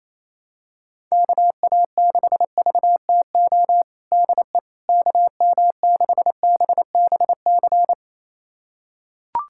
Morse code, heard as a keyed tone, about 28 wpm, 700 Hz. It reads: KA6VTO DE KM6BBC
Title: USA Callsign DE Callsign 28wpm